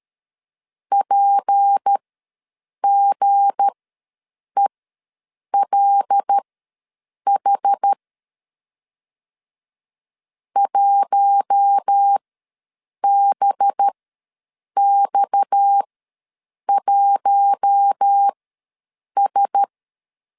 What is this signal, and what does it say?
Morse code, heard as a keyed tone, about 13 wpm, spaced as for 7 wpm, 800 Hz. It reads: PGELH 1BX1S
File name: LEZIONI / 7 GF9Z / 7 G ws 12wpm 3140